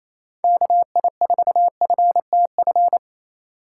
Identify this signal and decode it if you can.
Morse code, heard as a keyed tone, about 28 wpm, 700 Hz. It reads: KI4FTF
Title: USA Call Signs - 6 Characters - 4x Character Spacing - 28wpm